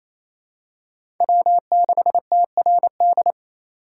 Morse code, heard as a keyed tone, about 28 wpm, 700 Hz. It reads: W6TRD
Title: USA Call Signs - 5 Characters - 4x Character Spacing - 28wpm